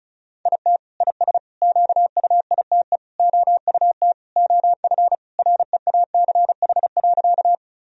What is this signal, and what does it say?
Morse code, heard as a keyed tone, about 36 wpm, 700 Hz. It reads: IT IS QUITE OUT OF REACH.
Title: Sentences from Top 1000 Words - Review Part 2 - 35wpm